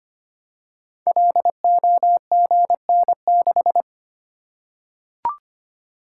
Morse code, heard as a keyed tone, about 25 wpm, 700 Hz. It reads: LOGN6
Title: Code Groups - 5 Letter:Numbers - 25wpm